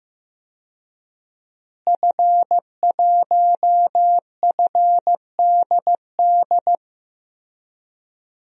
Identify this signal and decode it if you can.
Morse code, heard as a keyed tone, about 15 wpm, 700 Hz. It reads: F1FDD
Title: DXCC-100 Call Signs - 5 Characters - Repeated 2x - 15wpm